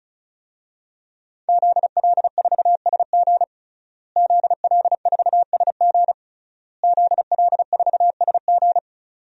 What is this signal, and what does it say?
Morse code, heard as a keyed tone, about 36 wpm, 700 Hz. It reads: ZL4SG ZL4SG ZL4SG
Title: DXCC-100 Call Signs - 5 Characters - Repeated 3x - 35wpm